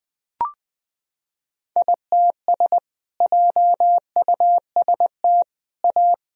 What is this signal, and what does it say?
Morse code, heard as a keyed tone, about 20 wpm, 700 Hz. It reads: ITS JUST A
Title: Sentences from Top 1000 Words - Review Part 4 - 20wpm